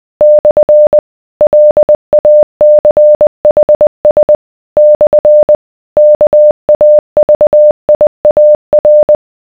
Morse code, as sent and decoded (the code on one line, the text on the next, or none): -..-. / .-.. .- -.-. .... ... / -..-. / -.- .- ...- .. .- .-.
/ LACHS / KAVIAR